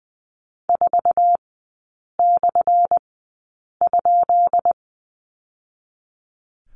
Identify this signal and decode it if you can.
Morse code, heard as a keyed tone, about 20 wpm, spaced as for 8 wpm, 700 Hz. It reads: 4/?
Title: 178 Lesson 29e - Slash - Three Characters 20wpm